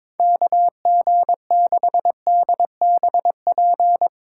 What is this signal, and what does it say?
Morse code, heard as a keyed tone, about 21 wpm, 700 Hz. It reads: KG6DBP
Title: USA Call Signs - 6 Characters - 3x Character Spacing - 22wpm